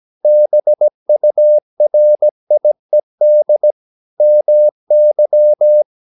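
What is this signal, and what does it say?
Morse code, heard as a keyed tone, about 17 wpm, 600 Hz. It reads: BURIED MY
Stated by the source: 17 wpm Morse practice sending the English text of a novel.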